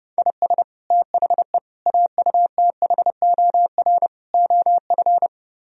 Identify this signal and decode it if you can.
Morse code, heard as a keyed tone, about 31 wpm, 700 Hz. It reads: IS THE AUTHOR OF